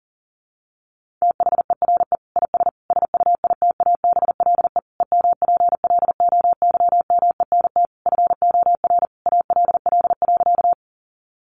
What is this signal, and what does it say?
Morse code, heard as a keyed tone, about 40 wpm, 700 Hz. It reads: THERE IS SUITABLE EMPLOYMENT FOR ALL.
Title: Sentences from Top 1500 Words 40wpm